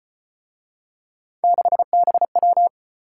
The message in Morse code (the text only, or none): 6BW